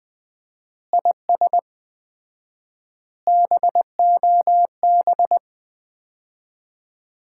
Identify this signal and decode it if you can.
Morse code, heard as a keyed tone, about 20 wpm, 700 Hz. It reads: IS BOB
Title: Common QSO Phrases - 4x Word-Spacing - 20wpm